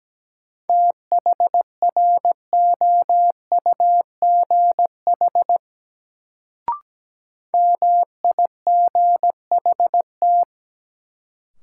Morse code, heard as a keyed tone, about 17 wpm, 700 Hz. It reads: THROUGH MIGHT
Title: Sets of 2 Words - New Word Drill - 17wpm